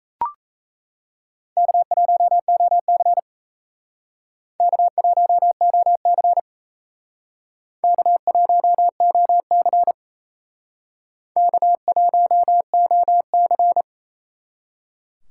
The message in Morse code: K1OC K1OC K1OC K1OC

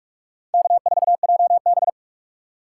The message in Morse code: K4JB

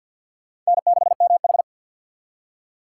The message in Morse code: N6MH

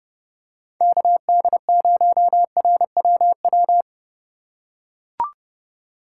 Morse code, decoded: KD0RWW

